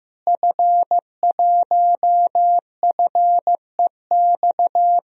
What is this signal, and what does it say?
Morse code, heard as a keyed tone, about 15 wpm, 700 Hz. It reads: F1FEX
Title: DXCC-100 Call Signs - 5 Characters - Repeated 2x - 15wpm